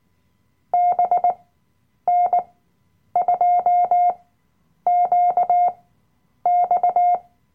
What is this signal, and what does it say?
Morse code, heard as a keyed tone, about 19 wpm, spaced as for 9 wpm, 710 Hz. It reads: BN2QX
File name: Workshop Dateien / CW-Online-Kurs / Lektion 007